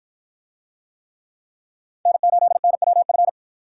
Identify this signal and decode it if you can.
Morse code, heard as a keyed tone, about 53 wpm, 700 Hz. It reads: N8NWF